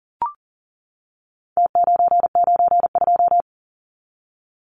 Morse code, heard as a keyed tone, about 40 wpm, 700 Hz. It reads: T992
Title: Numbers - Contest Style - 4-Digit 40wpm